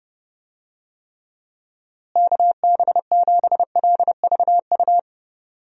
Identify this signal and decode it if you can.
Morse code, heard as a keyed tone, about 30 wpm, 700 Hz. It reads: KB7LVU
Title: USA Call Signs - 6 Characters - Repeated 3x - 30wpm